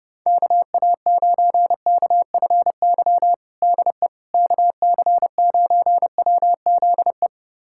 Morse code, heard as a keyed tone, about 30 wpm, 700 Hz. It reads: KA9KFY DE KC9WZE
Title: USA Callsign DE Callsign - 6 Characters - 30wpm